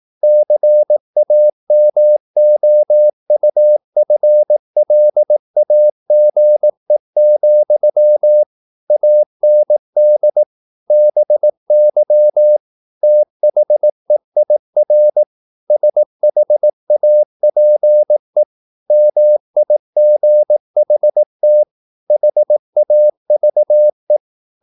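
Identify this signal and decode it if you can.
Morse code, heard as a keyed tone, about 18 wpm, 600 Hz. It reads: CAMOUFLAGE, AND BY THEIR SHAPE MIGHT HAVE
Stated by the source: Morse practice at 18 wpm, English text, from CW med Gnister.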